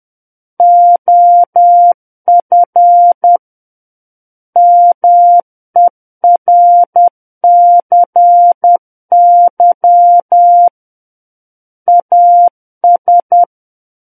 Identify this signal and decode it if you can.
Morse code, heard as a keyed tone, about 10 wpm, 700 Hz. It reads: OF MERCY AS
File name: Book One / 01-Chapter 1 - 10 WPM